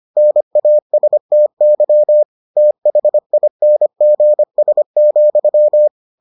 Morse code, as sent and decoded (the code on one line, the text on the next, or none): -. .- ... - -.-- / - .... .. -. --. ... --..--
NASTY THINGS,